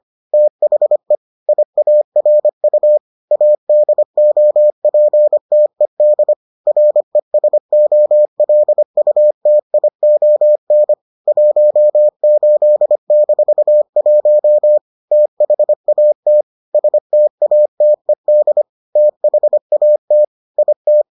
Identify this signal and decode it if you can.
Morse code, heard as a keyed tone, about 25 wpm, 600 Hz. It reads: THE IARU ADOPTED RESOLUTION 18-1 THAT STATED THAT IT